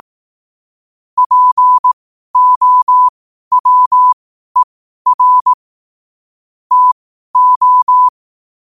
Morse code, heard as a keyed tone, about 18 wpm, spaced as for 12 wpm, 1000 Hz. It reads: POWER TO
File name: qotdfiles / 12wpm